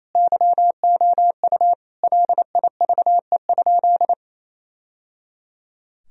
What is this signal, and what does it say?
Morse code, heard as a keyed tone, about 28 wpm, 700 Hz. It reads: YOU LIVE?
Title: Sentences from Top 1000 Words - Review Part 4 - 28wpm